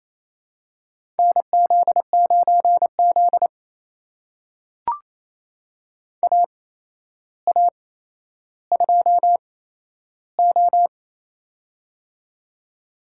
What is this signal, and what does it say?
Morse code, heard as a keyed tone, about 28 wpm, spaced as for 12 wpm, 700 Hz. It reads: NZ9Z AA2O
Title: USA Call Signs - 8x Character Spacing - 28wpm